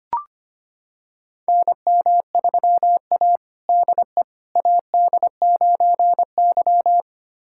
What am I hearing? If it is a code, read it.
Morse code, heard as a keyed tone, about 25 wpm, 700 Hz. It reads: NM3A DE AD9Y
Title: USA Callsign DE Callsign - 4 Characters - 25wpm